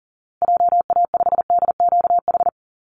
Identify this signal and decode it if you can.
Morse code, heard as a keyed tone, about 40 wpm, 700 Hz. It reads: JA5DQH
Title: Call Signs Repeated 3x - 40wpm